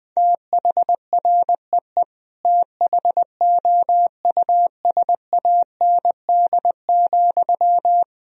Morse code, heard as a keyed tone, about 21 wpm, 700 Hz. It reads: THREE THOUSAND,
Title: Sentences from Top 1000 Words - Review Part 3 - 20wpm